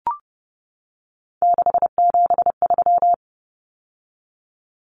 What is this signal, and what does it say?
Morse code, heard as a keyed tone, about 30 wpm, 700 Hz. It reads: 673
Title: Common Cut Numbers - 3-Digit 30wpm